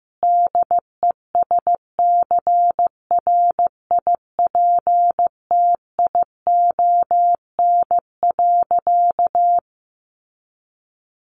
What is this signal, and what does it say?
Morse code, heard as a keyed tone, about 15 wpm, 700 Hz. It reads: DESCRIPTION.
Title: Sentences from Top 1500 Words 15wpm